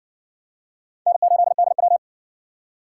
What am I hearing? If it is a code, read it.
Morse code, heard as a keyed tone, about 60 wpm, 700 Hz. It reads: N8DW